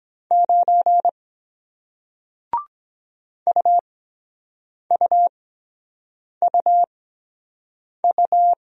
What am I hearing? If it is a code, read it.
Morse code, heard as a keyed tone, about 23 wpm, spaced as for 7 wpm, 700 Hz. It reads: 9 UUUU